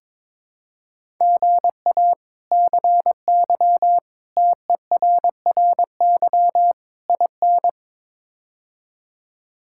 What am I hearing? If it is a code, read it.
Morse code, heard as a keyed tone, about 22 wpm, 700 Hz. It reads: GA CY TERRY IN
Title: K1USN SST QSO Practice - 22wpm